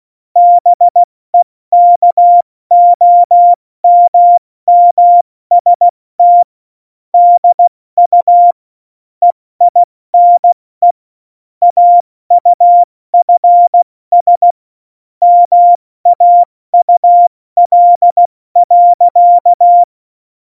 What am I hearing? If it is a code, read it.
Morse code, heard as a keyed tone, about 16 wpm, 700 Hz. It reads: BEKOMMST DU EINE AUFS MAUL.
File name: Morsekurs / Witze / Witz 22. Dezember 2020, 70 F80